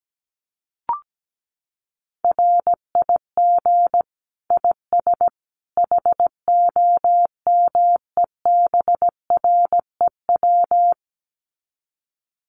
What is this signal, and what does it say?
Morse code, heard as a keyed tone, about 17 wpm, 700 Hz. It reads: RIG IS HOMEBREW
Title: Common QSO Phrases 17wpm